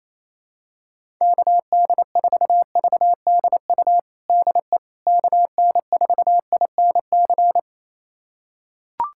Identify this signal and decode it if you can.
Morse code, heard as a keyed tone, about 28 wpm, 700 Hz. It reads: KD4VDU DE KN4INC
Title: USA Callsign DE Callsign - 6 Characters - 28wpm